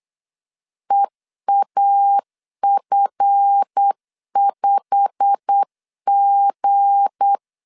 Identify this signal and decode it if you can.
Morse code, heard as a keyed tone, about 8 wpm, 800 Hz. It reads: EAF5G
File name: CW / DET / 1 9 1500C S B 8wpm 5624